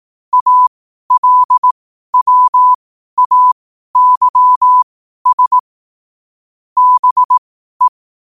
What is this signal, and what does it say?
Morse code, heard as a keyed tone, about 18 wpm, spaced as for 12 wpm, 1000 Hz. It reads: ALWAYS BE
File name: qotdfiles / 12wpm